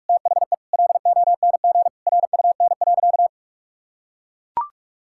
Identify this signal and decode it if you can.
Morse code, heard as a keyed tone, about 44 wpm, 700 Hz. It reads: THE LONG RUN.
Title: Sentences from Top 1000 Words - Review Part 6 - 45wpm